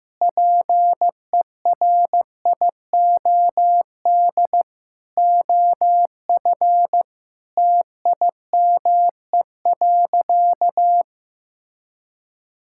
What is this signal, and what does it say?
Morse code, heard as a keyed tone, about 15 wpm, 700 Hz. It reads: PERIOD OF TIME.
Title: Sentences from Top 1000 Words - Review Part 5 - 15wpm